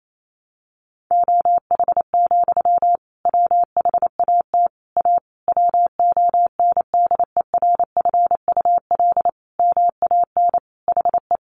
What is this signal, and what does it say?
Morse code, heard as a keyed tone, about 28 wpm, 700 Hz. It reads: OH, WHAT A WONDERFUL MAN HE
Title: Sentences from Top 1100 Words 28wpm